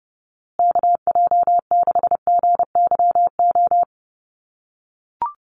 Morse code, heard as a keyed tone, about 30 wpm, 700 Hz. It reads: KJ6GYO
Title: USA Call Signs Repeated 2x 30wpm